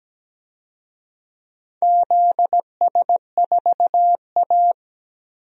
Morse code, heard as a keyed tone, about 17 wpm, 700 Hz. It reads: ZS4A
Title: DXCC-100 Call Signs - 4 Characters - 17wpm